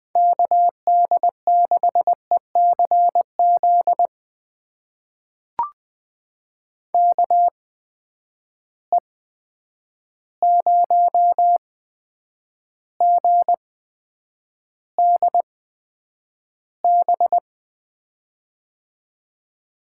Morse code, heard as a keyed tone, about 20 wpm, spaced as for 9 wpm, 700 Hz. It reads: KD6ECZ KE0GDB
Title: USA Call Signs - 6 Characters - 8x Character Spacing - 20wpm